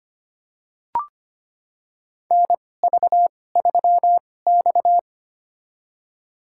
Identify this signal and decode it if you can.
Morse code, heard as a keyed tone, about 25 wpm, spaced as for 18 wpm, 700 Hz. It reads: NV3X
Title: USA Call Signs - 4 Characters - 2x Character Spacing - 25wpm